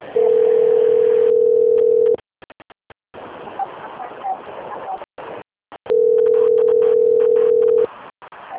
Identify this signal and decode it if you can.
Ringback tone (North America), 440 + 480 Hz.
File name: out- - - - - .26422